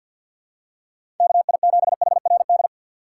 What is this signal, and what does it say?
Morse code, heard as a keyed tone, about 52 wpm, 700 Hz. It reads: KI7HRD
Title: USA Call Signs - 6 Characters - 6x Character Spacing - 50wpm